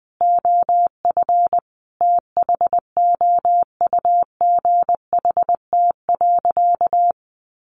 Morse code, heard as a keyed tone, about 20 wpm, 700 Hz. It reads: OF THOUGHT.
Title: Sentences from Top 1300 Words 20wpm